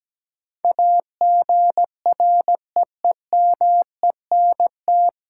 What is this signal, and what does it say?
Morse code, heard as a keyed tone, about 17 wpm, 700 Hz. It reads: AGREEMENT
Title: Sentences from Top 1500 Words - New Word Drill Alt - 17wpm